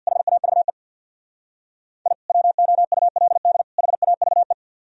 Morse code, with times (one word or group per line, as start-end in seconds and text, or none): 0.1-0.7 HAVE
2.1-2.1 I
2.3-3.6 WOULD
3.8-4.5 HAVE